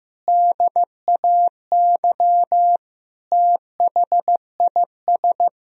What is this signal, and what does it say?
Morse code, heard as a keyed tone, about 16 wpm, 700 Hz. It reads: DAY THIS
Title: Sentences from Top 1000 Words - Review Part 2 - 15wpm